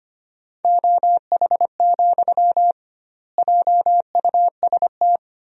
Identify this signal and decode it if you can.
Morse code, heard as a keyed tone, about 25 wpm, 700 Hz. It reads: OH, JUST
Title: 2x Word-Spacing - Sentences from Top 500 Words - 25wpm